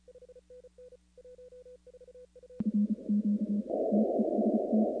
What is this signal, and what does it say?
Morse code, heard as a keyed tone, about 30 wpm, 250 Hz. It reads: FQ TEST
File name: contests / 2019-cw-wpx-kt5j-k5pi- -2330 0-21